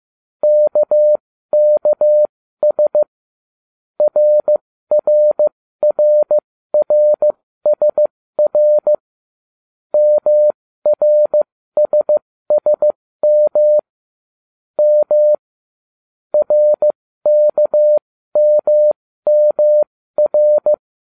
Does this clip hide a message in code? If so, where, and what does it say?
Yes: Morse code from 0.4 to 20.7 s, heard as a keyed tone, about 15 wpm, spaced as for 13 wpm, 600 Hz. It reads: KKS RRRRSR MRSSM M RKMMR